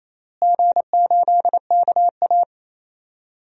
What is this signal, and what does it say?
Morse code, heard as a keyed tone, about 27 wpm, 700 Hz. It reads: G8KA
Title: DXCC-100 Call Signs - 4 Characters - Repeated 2x - 28wpm